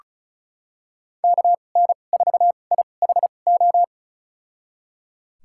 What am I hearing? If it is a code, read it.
Morse code, heard as a keyed tone, about 36 wpm, spaced as for 25 wpm, 700 Hz. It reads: KN4IHO
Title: USA Call Signs - 6 Characters - 2x Character Spacing - 35wpm